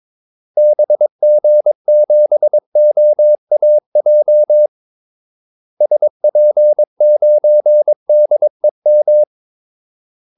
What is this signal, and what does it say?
Morse code, heard as a keyed tone, about 22 wpm, 600 Hz. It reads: BG7OAJ SP9DEM
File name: Callsigns-22wpm